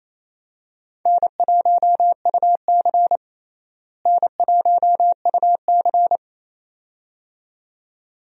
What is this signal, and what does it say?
Morse code, heard as a keyed tone, about 28 wpm, 700 Hz. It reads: N1UC N1UC
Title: USA Call Signs - 4 Characters - Repeated 2x - 28wpm